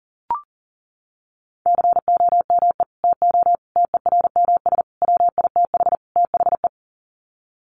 Morse code, heard as a keyed tone, about 40 wpm, 700 Hz. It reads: COME TO TERMS WITH THE
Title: Sets of 5 Words Encore2 40wpm